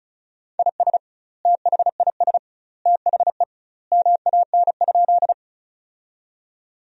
Morse code, heard as a keyed tone, about 34 wpm, 700 Hz. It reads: IS THIS THE MAN?
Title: 2x Word-Spacing - Sentences from Top 200 Words - 35wpm